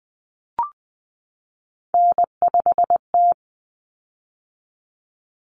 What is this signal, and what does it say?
Morse code, heard as a keyed tone, about 20 wpm, 700 Hz. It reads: N5T
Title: Common Cut Numbers - 3-Digit 20wpm